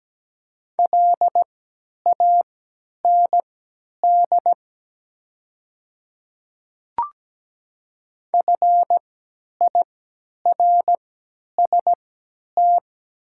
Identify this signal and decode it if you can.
Morse code, heard as a keyed tone, about 17 wpm, spaced as for 10 wpm, 700 Hz. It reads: LAND FIRST
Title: 3x Character Spacing - Top 200 Words - 17wpm